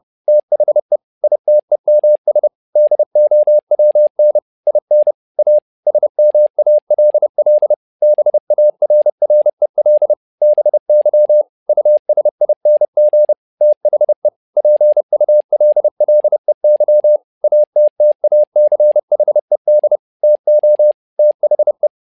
Morse code, heard as a keyed tone, about 30 wpm, 600 Hz. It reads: THE ITEMS DOWN IN A SMALL BARREL BY USING THE PULLEY ATTACHED TO THE
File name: learning center / tower trouble / tower trouble 30wpm